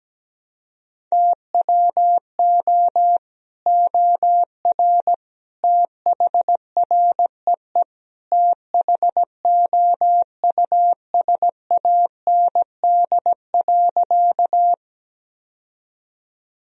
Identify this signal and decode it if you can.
Morse code, heard as a keyed tone, about 17 wpm, 700 Hz. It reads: TWO OR THREE THOUSAND.
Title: Sentences from Top 1000 Words - Review Part 3 - 17wpm